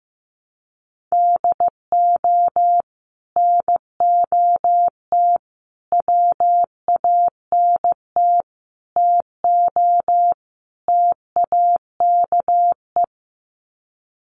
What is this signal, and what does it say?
Morse code, heard as a keyed tone, about 15 wpm, 700 Hz. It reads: DO NOT WANT TO TAKE
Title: Sets of 5 Words Encore2 15wpm